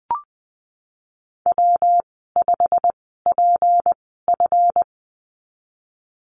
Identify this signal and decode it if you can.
Morse code, heard as a keyed tone, about 20 wpm, spaced as for 15 wpm, 700 Hz. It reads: W5PF
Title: 161 Lesson 26g - Letter g - Call Signs 20wpm